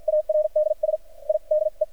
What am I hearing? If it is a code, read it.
Morse code, heard as a keyed tone, about 51 wpm, spaced as for 43 wpm, 610 Hz. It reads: ARDS I DE
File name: archived / CRSnet